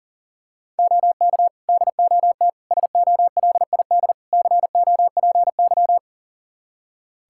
Morse code, heard as a keyed tone, about 40 wpm, 690 Hz. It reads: OK DOT SOLID COPY
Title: Practice QSO - Exchange 3 - More Info - 40wpm